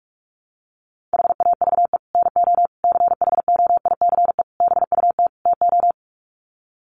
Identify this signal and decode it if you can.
Morse code, heard as a keyed tone, about 44 wpm, 700 Hz. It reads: HAVE NO CHOICE BUT TO
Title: Sets of 5 Words Encore 45wpm